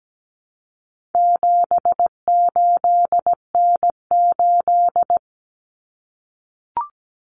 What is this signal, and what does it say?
Morse code, heard as a keyed tone, about 17 wpm, 700 Hz. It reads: 78N8
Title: Common Cut Numbers - 4-Digit 17wpm